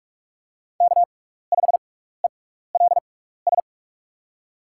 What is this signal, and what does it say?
Morse code, heard as a keyed tone, about 44 wpm, spaced as for 15 wpm, 700 Hz. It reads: K5ELS
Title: USA Call Signs - 5 Characters - 6x Character Spacing - 45wpm